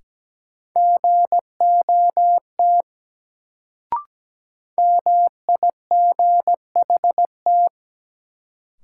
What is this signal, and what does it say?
Morse code, heard as a keyed tone, about 17 wpm, 700 Hz. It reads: GOT MIGHT
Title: Sentences from Top 300 Words - New Word Drill - 17wpm